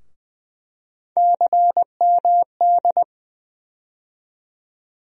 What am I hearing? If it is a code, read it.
Morse code, heard as a keyed tone, about 20 wpm, 700 Hz. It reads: CMD
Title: USA Call Sign Postfixes - 3 Letters 20wpm